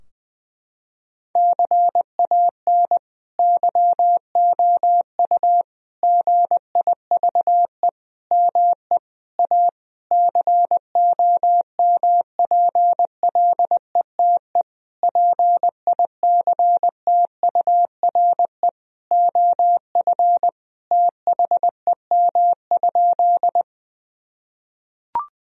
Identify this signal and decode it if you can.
Morse code, heard as a keyed tone, about 20 wpm, 700 Hz. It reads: CAN YOU GIVE ME A COMPLETE PICTURE OF THEM?
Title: Sentences from Top 1000 Words - Review Part 2 - 20wpm